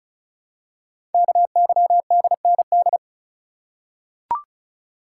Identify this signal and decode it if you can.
Morse code, heard as a keyed tone, about 36 wpm, 690 Hz. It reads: KYDND
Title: Code Groups - 5 Letters - 35wpm